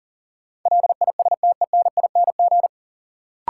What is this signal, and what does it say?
Morse code, heard as a keyed tone, about 40 wpm, 700 Hz. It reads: LISTENING
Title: Sets of 5 Words - New Word Drill - 40wpm